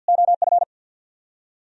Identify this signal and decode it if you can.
Morse code, heard as a keyed tone, about 64 wpm, 700 Hz. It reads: OF